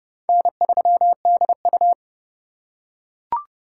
Morse code, heard as a keyed tone, about 30 wpm, 700 Hz. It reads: N3DU
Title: USA Call Signs - 2x Character Spacing - 30wpm